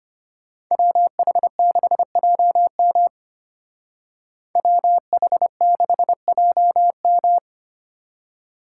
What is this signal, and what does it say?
Morse code, heard as a keyed tone, about 33 wpm, 700 Hz. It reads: WH6JM WH6JM